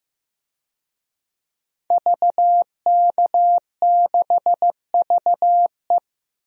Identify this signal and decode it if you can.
Morse code, heard as a keyed tone, about 15 wpm, 700 Hz. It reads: VK6VE
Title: DXCC-100 Call Signs - 5 Characters - 15wpm